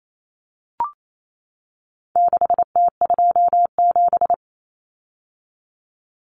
Morse code, heard as a keyed tone, about 28 wpm, 700 Hz. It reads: 6T27
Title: Common Cut Numbers - 4-Digit 28wpm